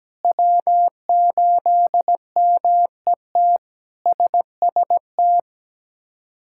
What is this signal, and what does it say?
Morse code, heard as a keyed tone, about 17 wpm, 700 Hz. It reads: W8MET SST
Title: K1USN SST QSO Practice - 17wpm